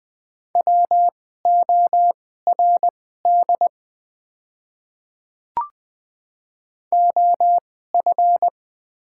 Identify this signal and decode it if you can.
Morse code, heard as a keyed tone, about 20 wpm, spaced as for 14 wpm, 700 Hz. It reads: WORD OF
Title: 2x Character Spacing - Top 100 Words - 20wpm